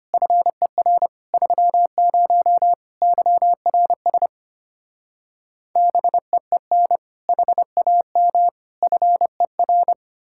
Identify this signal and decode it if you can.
Morse code, heard as a keyed tone, about 33 wpm, 700 Hz. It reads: FER 30 YRS BEEN HAM FER